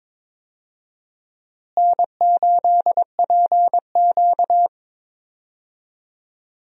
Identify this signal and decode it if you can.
Morse code, heard as a keyed tone, about 22 wpm, 700 Hz. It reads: N8PQ
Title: USA Call Signs - 4 Characters - Repeated 2x - 22wpm